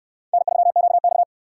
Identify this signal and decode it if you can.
Morse code, heard as a keyed tone, about 60 wpm, 700 Hz. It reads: N3QX